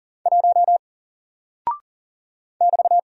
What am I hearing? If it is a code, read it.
Morse code, heard as a keyed tone, about 40 wpm, 700 Hz. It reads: 1 <BT>
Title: Single Character (Letter:Number:Punctuation:Prosign) 40wpm